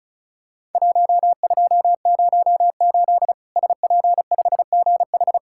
Morse code, heard as a keyed tone, about 36 wpm, 700 Hz. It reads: 1208 SP5GH
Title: Worked All Europe DX Contest - QTC Exchange - 35wpm